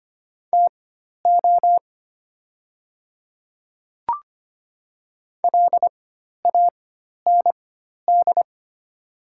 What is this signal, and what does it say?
Morse code, heard as a keyed tone, about 25 wpm, spaced as for 12 wpm, 700 Hz. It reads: TO LAND
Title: 4x Character Spacing - Top 200 Words - 25wpm